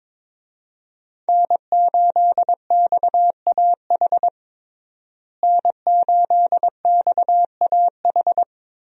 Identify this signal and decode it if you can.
Morse code, heard as a keyed tone, about 22 wpm, 700 Hz. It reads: N8XAH N8XAH
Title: USA Call Signs - 5 Characters - Repeated 3x - 22wpm